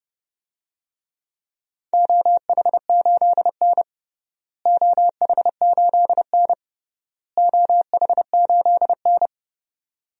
Morse code, heard as a keyed tone, about 32 wpm, 700 Hz. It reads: OH8N OH8N OH8N
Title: DXCC-100 Call Signs - 4 Characters - Repeated 3x - 30wpm